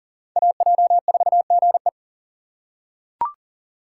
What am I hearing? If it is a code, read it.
Morse code, heard as a keyed tone, about 38 wpm, 700 Hz. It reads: AJ4GE
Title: USA Call Signs - 2x Character Spacing - 40wpm